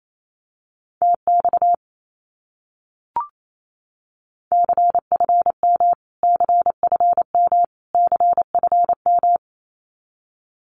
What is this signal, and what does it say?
Morse code, heard as a keyed tone, about 28 wpm, 700 Hz. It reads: TX CFM CFM CFM
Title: Abbreviations Repeated 3x - 28wpm